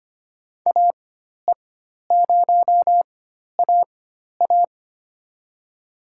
Morse code, heard as a keyed tone, about 25 wpm, spaced as for 12 wpm, 700 Hz. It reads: AE0AA
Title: USA Call Signs - 5 Characters - 4x Character Spacing - 25wpm